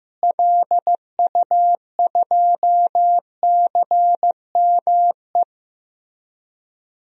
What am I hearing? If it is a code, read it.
Morse code, heard as a keyed tone, about 15 wpm, 700 Hz. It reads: LU2CME
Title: DXCC-100 Call Signs - 6 Characters - Repeated 3x - 15wpm